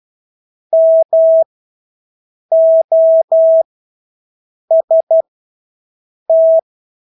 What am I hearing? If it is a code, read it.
Morse code, heard as a keyed tone, about 12 wpm, spaced as for 6 wpm, 650 Hz. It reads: MOST
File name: Engelske ord 12-6 wpm